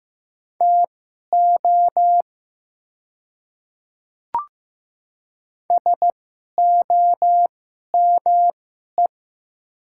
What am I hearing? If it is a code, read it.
Morse code, heard as a keyed tone, about 15 wpm, spaced as for 11 wpm, 700 Hz. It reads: TO SOME
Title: 2x Character Spacing - Top 100 Words - 15wpm